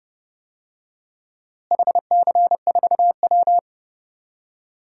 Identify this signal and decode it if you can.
Morse code, heard as a keyed tone, about 30 wpm, 700 Hz. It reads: HC4W
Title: DXCC-100 Call Signs - 4 Characters - Repeated 3x - 30wpm